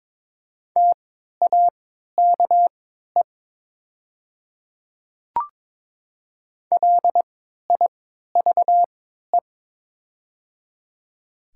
Morse code, heard as a keyed tone, about 22 wpm, spaced as for 12 wpm, 700 Hz. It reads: TAKE LIVE
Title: 3x Character Spacing - Top 300 Words - 22wpm